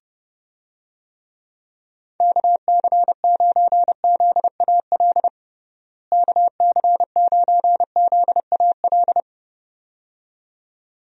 Morse code, heard as a keyed tone, about 32 wpm, 700 Hz. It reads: KC9ZAL KC9ZAL
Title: USA Call Signs - 6 Characters - Repeated 2x - 30wpm